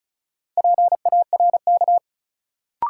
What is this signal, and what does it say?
Morse code, heard as a keyed tone, about 34 wpm, 700 Hz. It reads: PARK
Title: Sentences from Top 1300 Words - New Word Drill Alt - 35wpm